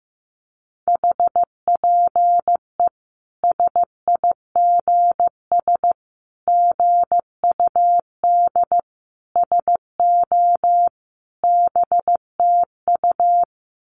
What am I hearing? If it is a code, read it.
Morse code, heard as a keyed tone, about 15 wpm, 700 Hz. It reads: HPE SIGS GUD SO BTU
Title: Common QSO Phrases 15wpm